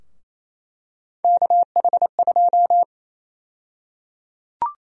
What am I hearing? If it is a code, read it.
Morse code, heard as a keyed tone, about 27 wpm, 700 Hz. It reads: KH2
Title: USA Call Sign Prefixes - Repeated 2x - 28wpm